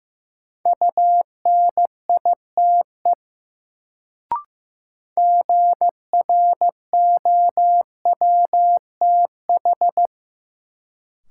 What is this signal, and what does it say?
Morse code, heard as a keyed tone, about 15 wpm, 700 Hz. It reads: UNITE GROWTH
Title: Sentences from Top 1000 Words - Review Part 5 - New Word Drill - 15wpm